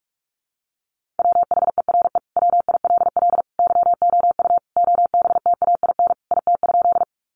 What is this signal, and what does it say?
Morse code, heard as a keyed tone, about 44 wpm, 700 Hz. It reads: WHERE WILL YOU OBTAIN IT?
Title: Sentences from Top 1300 Words 45wpm